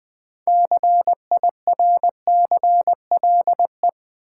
Morse code, heard as a keyed tone, about 20 wpm, 700 Hz. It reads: CIRCLE
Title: 6 Letter Words - First 3 Letters Given - 20wpm